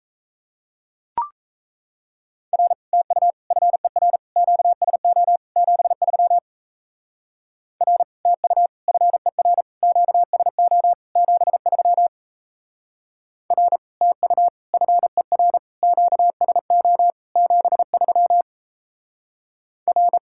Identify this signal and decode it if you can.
Morse code, heard as a keyed tone, about 37 wpm, 700 Hz. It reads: R TU FER QSO 73 R TU FER QSO 73 R TU FER QSO 73 R